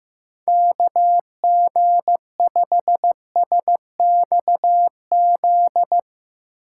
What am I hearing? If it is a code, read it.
Morse code, heard as a keyed tone, about 16 wpm, 700 Hz. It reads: KG5SXZ
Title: Speed Racing - USA 2x3 Call Signs 15wpm